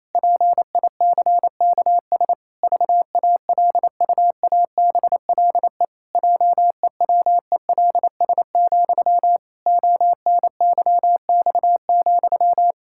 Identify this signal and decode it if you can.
Morse code, heard as a keyed tone, about 28 wpm, 700 Hz. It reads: PICKS VALUABLE JEWELS, ONYX,
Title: Pangram Challenge v2 - 28wpm